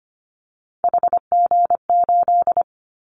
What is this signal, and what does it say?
Morse code, heard as a keyed tone, about 25 wpm, 700 Hz. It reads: HG8
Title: Callsign Prefixes for Top 50 DXCC - 25wpm